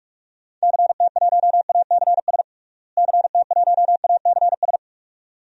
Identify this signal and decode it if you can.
Morse code, heard as a keyed tone, about 46 wpm, 700 Hz. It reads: CT1ACS CT1ACS
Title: DXCC-100 Call Signs - 6 Characters - Repeated 2x - 45wpm